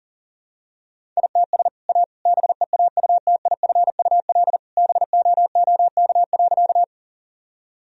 Morse code, heard as a keyed tone, about 37 wpm, 700 Hz. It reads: ITS A BEAUTIFUL BOOK.
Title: Sentences from Top 1000 Words - Review Part 1 - 40wpm